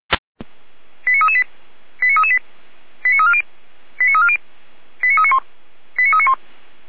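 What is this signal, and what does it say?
Five-tone selective call (ZVEI), heard as quick runs of five tones, tones of about 70 ms. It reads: 89338 89338 89340 89340 89391 89391
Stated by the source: Five-tone ZVEI selective call 89304, received by SDR.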